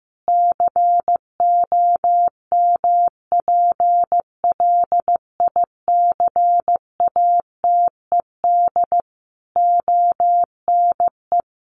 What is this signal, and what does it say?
Morse code, heard as a keyed tone, about 15 wpm, 700 Hz. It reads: COMPLICATED ONE
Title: Sentences from Top 1200 Words 15wpm